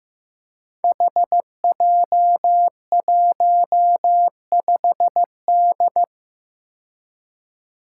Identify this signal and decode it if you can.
Morse code, heard as a keyed tone, about 15 wpm, 700 Hz. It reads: HJ15D
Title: Five Challenging Letter-Number Combinations 15wpm